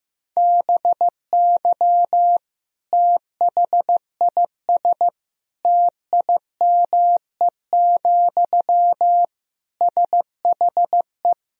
Morse code, heard as a keyed tone, about 16 wpm, 700 Hz. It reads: BY THIS TIME, SHE
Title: Sentences from Top 1000 Words - Review Part 4 - 15wpm